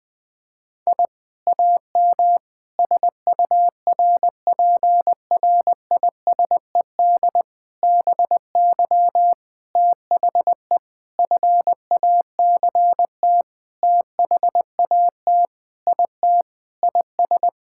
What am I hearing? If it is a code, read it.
Morse code, heard as a keyed tone, about 20 wpm, 700 Hz. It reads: I AM SURPRISED BY THE FACT THAT IT IS
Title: Sentences with 4-Word Phrases - 20wpm